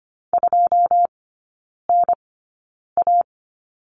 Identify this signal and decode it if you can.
Morse code, heard as a keyed tone, about 25 wpm, spaced as for 9 wpm, 690 Hz. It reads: 2NA
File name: Lesson 14e - Number 2 - Three Characters 25wpm